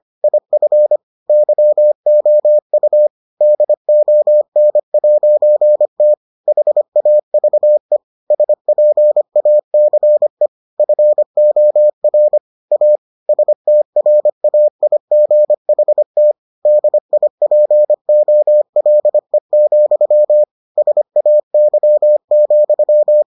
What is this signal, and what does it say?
Morse code, heard as a keyed tone, about 25 wpm, 600 Hz. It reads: IF YOU DON'T HAVE SPACE FOR A STRAIGHT DIPOLE, SAY,